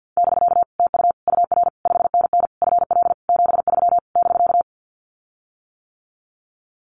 Morse code, heard as a keyed tone, about 48 wpm, 700 Hz. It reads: <BK> TU UR 5NN FL 73 <BK>
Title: Common QSO Phrases Encore 50wpm